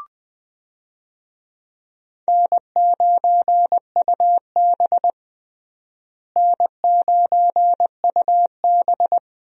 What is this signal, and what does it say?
Morse code, heard as a keyed tone, about 20 wpm, 700 Hz. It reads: N9UB N9UB
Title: USA Call Signs - 4 Characters - Repeated 3x - 20wpm